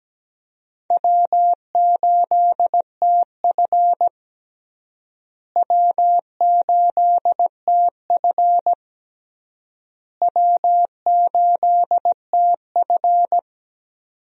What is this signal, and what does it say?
Morse code, heard as a keyed tone, about 17 wpm, 700 Hz. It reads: W8TF W8TF W8TF
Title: USA Call Signs - 4 Characters - Repeated 3x - 17wpm